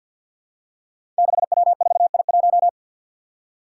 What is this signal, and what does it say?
Morse code, heard as a keyed tone, about 51 wpm, 700 Hz. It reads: 6W4I1
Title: Five Challenging Letter-Number Combinations 50wpm